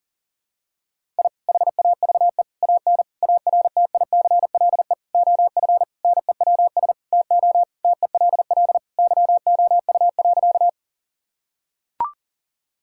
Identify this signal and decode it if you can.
Morse code, heard as a keyed tone, about 40 wpm, 700 Hz. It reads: I HAVE AN ARTICLE OF NEWS TO TELL YOU.
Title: Sentences from Top 1000 Words - Review Part 5 - 40wpm